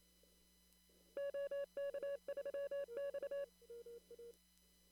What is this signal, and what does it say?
Morse code, heard as a keyed tone, about 27 wpm, 560 Hz. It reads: OK3X
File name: CW / kt5j wpx cw-op-k5pi